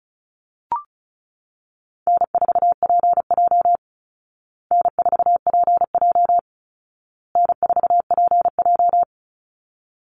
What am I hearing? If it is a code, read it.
Morse code, heard as a keyed tone, about 34 wpm, 700 Hz. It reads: N4PJ N4PJ N4PJ
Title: Easy Call Signs Repeated 3x - 35wpm